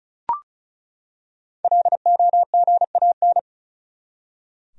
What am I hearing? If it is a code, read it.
Morse code, heard as a keyed tone, about 34 wpm, 700 Hz. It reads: LOGAN
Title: USA Common Names - 35wpm